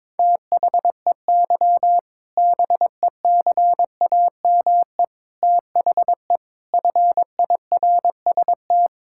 Morse code, heard as a keyed tone, about 21 wpm, 700 Hz. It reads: THEY BECAME THE FIRST
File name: practice / 22wpm / Sentences with 8-Word Phrases - 22wpm